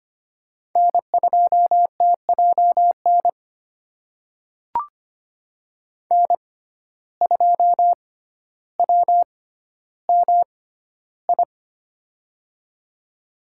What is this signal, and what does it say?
Morse code, heard as a keyed tone, about 25 wpm, spaced as for 13 wpm, 700 Hz. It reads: N2TJN N2WMI
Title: USA Call Signs - 5 Characters - 6x Character Spacing - 25wpm